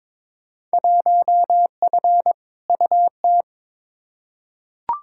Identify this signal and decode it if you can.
Morse code, heard as a keyed tone, about 21 wpm, 700 Hz. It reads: 1F UT
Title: Field Day Exchanges Encore - 22wpm